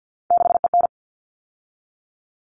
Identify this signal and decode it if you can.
Morse code, heard as a keyed tone, about 50 wpm, 700 Hz. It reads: 6EN
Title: All Cut Numbers - 3-Digit 50wpm